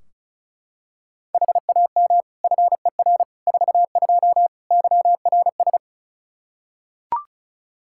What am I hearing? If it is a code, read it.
Morse code, heard as a keyed tone, about 36 wpm, 700 Hz. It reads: HAM FER 42 YRS
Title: Practice QSO - Exchange 3 - More Info - 35wpm